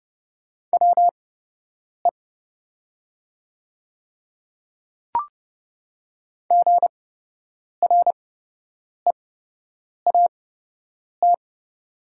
Morse code, heard as a keyed tone, about 30 wpm, spaced as for 8 wpm, 700 Hz. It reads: WE GREAT
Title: 8x Character Spacing - Top 200 Words - 30wpm